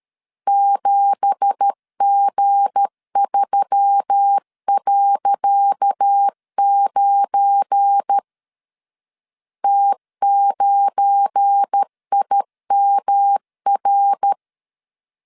Morse code, heard as a keyed tone, about 13 wpm, 800 Hz. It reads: 7G3.9 T9IMR
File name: LEZIONI / 10 SIGNS / 10C DOT 12wpm 2451